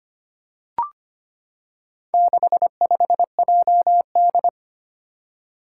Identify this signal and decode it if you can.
Morse code, heard as a keyed tone, about 25 wpm, 700 Hz. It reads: 65JD
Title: Four Challenging Letter-Number Combinations 25wpm